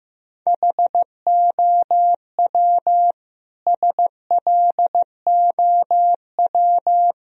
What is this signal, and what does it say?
Morse code, heard as a keyed tone, about 15 wpm, 700 Hz. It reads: HOW SLOW
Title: Sentences from Top 1000 Words - Review Part 2 - 15wpm